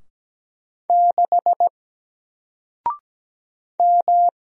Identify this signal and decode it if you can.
Morse code, heard as a keyed tone, about 17 wpm, 700 Hz. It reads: 6 M
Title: Single Letter-Number Alt 17wpm